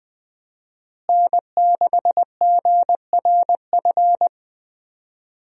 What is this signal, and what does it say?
Morse code, heard as a keyed tone, about 20 wpm, 700 Hz. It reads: N6GRF
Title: USA Call Signs - 5 Characters - Repeated 3x - 20wpm